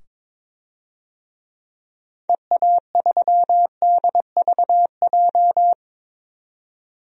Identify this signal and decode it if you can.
Morse code, heard as a keyed tone, about 22 wpm, 700 Hz. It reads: EA3DVJ
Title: DXCC-100 Call Signs - 6 Characters - 22wpm